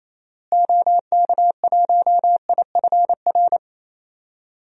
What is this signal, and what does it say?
Morse code, heard as a keyed tone, about 28 wpm, 700 Hz. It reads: OK1IFR
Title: DXCC-100 Call Signs - 6 Characters - Repeated 3x - 28wpm